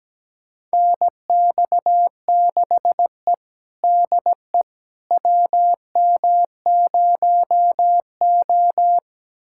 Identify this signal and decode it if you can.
Morse code, heard as a keyed tone, about 17 wpm, 700 Hz. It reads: NX6E DE WM0O
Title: USA Callsign DE Callsign - 4 Characters - 17wpm